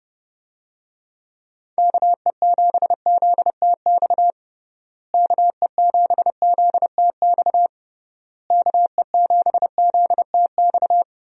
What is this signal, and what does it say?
Morse code, heard as a keyed tone, about 31 wpm, 700 Hz. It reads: KE7ZTX KE7ZTX KE7ZTX
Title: USA Call Signs - 6 Characters - Repeated 3x - 30wpm